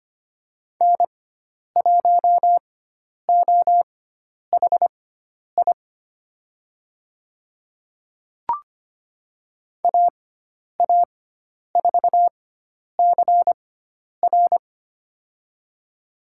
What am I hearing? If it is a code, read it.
Morse code, heard as a keyed tone, about 25 wpm, spaced as for 10 wpm, 700 Hz. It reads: N1OHI AA4CR
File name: Lesson 16g - Letter c - Call Signs 25wpm